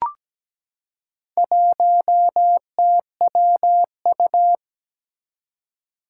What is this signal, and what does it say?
Morse code, heard as a keyed tone, about 17 wpm, 700 Hz. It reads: 1TWU
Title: Four Challenging Letter-Number Combinations 17wpm